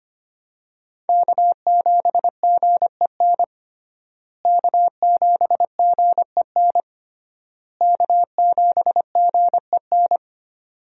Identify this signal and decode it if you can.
Morse code, heard as a keyed tone, about 25 wpm, 700 Hz. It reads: K7GEN K7GEN K7GEN
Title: USA Call Signs - 5 Characters - Repeated 3x - 25wpm